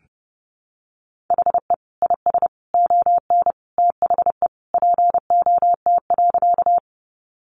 Morse code, heard as a keyed tone, about 28 wpm, 700 Hz. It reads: HE IS ON THE POT.
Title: Sentences from Top 1000 Words 30wpm